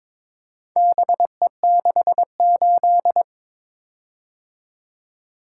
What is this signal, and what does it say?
Morse code, heard as a keyed tone, about 22 wpm, 700 Hz. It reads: BE68
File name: practice / 22wpm / Four Letter-Number Combinations - 22wpm